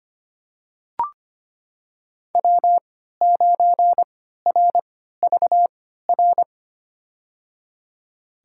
Morse code, heard as a keyed tone, about 25 wpm, spaced as for 14 wpm, 700 Hz. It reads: W9RVR
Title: USA Call Signs - 3x Character Spacing - 25wpm